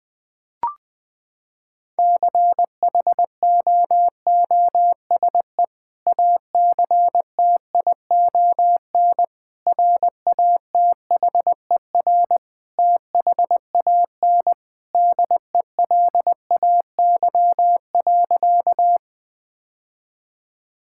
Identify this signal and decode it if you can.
Morse code, heard as a keyed tone, about 20 wpm, 700 Hz. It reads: CHOOSE ACTION RATHER THAN DELAY.
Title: Sentences with 2-Word Phrases - 20wpm